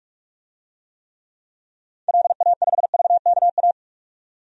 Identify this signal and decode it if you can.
Morse code, heard as a keyed tone, about 46 wpm, 700 Hz. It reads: LA5VKA
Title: DXCC-100 Call Signs - 6 Characters - 45wpm